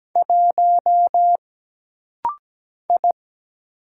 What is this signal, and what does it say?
Morse code, heard as a keyed tone, about 17 wpm, 700 Hz. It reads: 1 I
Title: Single Letter-Number - Rapid-Fire 17wpm